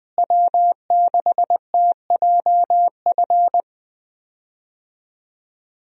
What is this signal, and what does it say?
Morse code, heard as a keyed tone, about 20 wpm, 700 Hz. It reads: W6TJF
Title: USA Call Signs - 5 Characters - Repeated 3x - 20wpm